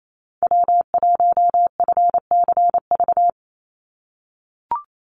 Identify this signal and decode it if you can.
Morse code, heard as a keyed tone, about 28 wpm, 700 Hz. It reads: W1FCV
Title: Call Signs - 28wpm